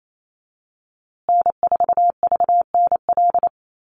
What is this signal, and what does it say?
Morse code, heard as a keyed tone, about 28 wpm, 700 Hz. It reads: N4VNL
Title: USA Call Signs Repeated 3x 28wpm